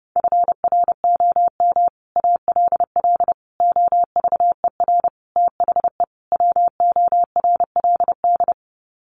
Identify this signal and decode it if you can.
Morse code, heard as a keyed tone, about 30 wpm, 700 Hz. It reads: FROM ALL OVER THE WORLD
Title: Sets of 5 Words 30wpm